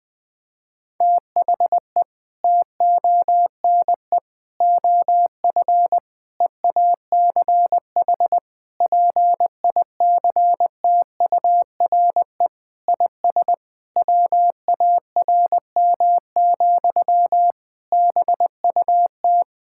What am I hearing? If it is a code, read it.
Morse code, heard as a keyed tone, about 20 wpm, 700 Hz. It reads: THE TONE OF EACH PICTURE IS WARM, BUT
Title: Sentences from Top 1600 Words 20wpm